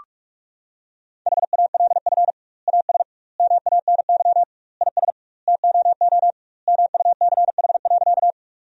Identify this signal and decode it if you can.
Morse code, heard as a keyed tone, about 48 wpm, 700 Hz. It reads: HALF AS MANY IS TOO MUCH.
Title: 2x Word-Spacing - Sentences from Top 400 Words - 45wpm